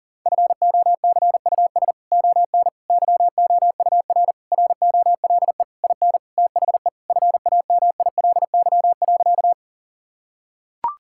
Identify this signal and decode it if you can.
Morse code, heard as a keyed tone, about 37 wpm, 700 Hz. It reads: FOCUS ON YOUR ROLE IN THE FAMILY.
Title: Sentences from Top 1000 Words - Review Part 6 - 40wpm